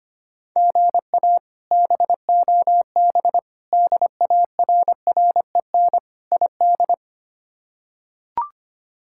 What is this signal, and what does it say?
Morse code, heard as a keyed tone, about 25 wpm, 700 Hz. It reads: GA BOB DARREN ID
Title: K1USN SST QSO Practice - 25wpm